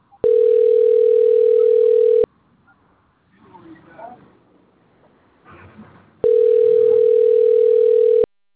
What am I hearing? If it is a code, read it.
Ringback tone (North America), 440 + 480 Hz.